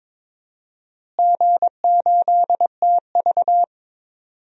Morse code, heard as a keyed tone, about 22 wpm, 700 Hz. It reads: G8TV
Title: DXCC-100 Call Signs - 4 Characters - Repeated 3x - 22wpm